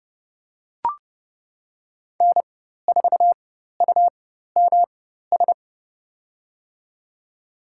Morse code, heard as a keyed tone, about 31 wpm, spaced as for 14 wpm, 700 Hz. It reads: N4UMS
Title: USA Call Signs - 5 Characters - 4x Character Spacing - 30wpm